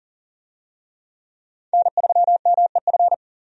Morse code, heard as a keyed tone, about 40 wpm, 700 Hz. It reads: N3MEF
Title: USA Call Signs - 5 Characters - 40wpm